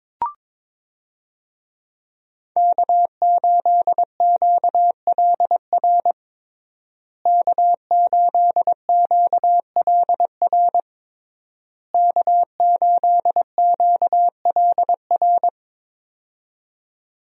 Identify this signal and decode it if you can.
Morse code, heard as a keyed tone, about 22 wpm, 700 Hz. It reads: K8QLR K8QLR K8QLR
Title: USA Call Signs - 5 Characters - Repeated 3x - 22wpm